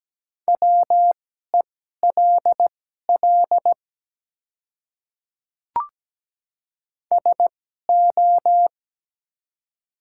Morse code, heard as a keyed tone, about 17 wpm, spaced as for 12 wpm, 700 Hz. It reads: WELL SO
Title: 2x Character Spacing - Top 200 Words - 17wpm